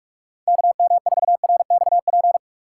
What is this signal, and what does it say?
Morse code, heard as a keyed tone, about 58 wpm, 700 Hz. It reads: KM4RXP